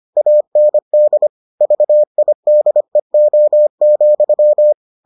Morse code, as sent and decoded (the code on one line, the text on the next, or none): .- -. -.. / ...- .. -.. --- --..--
AND VIDEO,